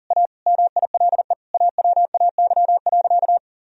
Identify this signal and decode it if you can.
Morse code, heard as a keyed tone, about 37 wpm, 700 Hz. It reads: A MILE AWAY.
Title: Sentences from Top 1000 Words - Review Part 1 - 40wpm